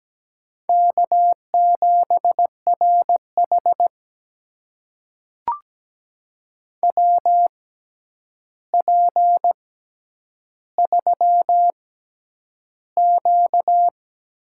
Morse code, heard as a keyed tone, about 17 wpm, spaced as for 9 wpm, 700 Hz. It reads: K7RH WP3Q
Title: USA Call Signs - 4 Characters - 6x Character Spacing - 17wpm